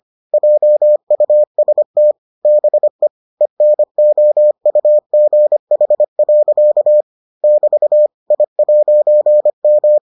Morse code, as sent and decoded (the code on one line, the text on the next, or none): .--- ..- ... - / -... / . -. --- ..- --. .... .-.-.- / -...- / .. .----. --
JUST BE ENOUGH. <BT> I'M